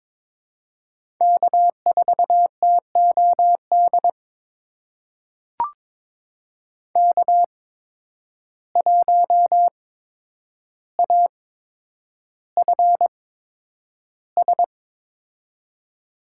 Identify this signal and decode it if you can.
Morse code, heard as a keyed tone, about 22 wpm, spaced as for 9 wpm, 700 Hz. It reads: K4TOD K1AFS
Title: USA Call Signs - 5 Characters - 8x Character Spacing - 22wpm